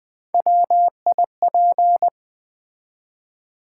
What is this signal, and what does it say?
Morse code, heard as a keyed tone, about 20 wpm, 690 Hz. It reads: WIP
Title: Build A Word - 1 Letter At A Time - 4-Letter Words - 20wpm